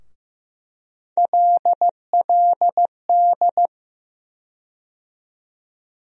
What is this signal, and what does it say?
Morse code, heard as a keyed tone, about 15 wpm, 700 Hz. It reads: LLD
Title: USA Call Sign Postfixes - 3 Letters 15wpm